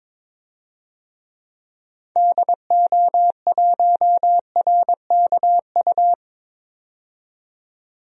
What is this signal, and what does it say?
Morse code, heard as a keyed tone, about 22 wpm, 700 Hz. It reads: DO1RKU
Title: DXCC-100 Call Signs - 6 Characters - 22wpm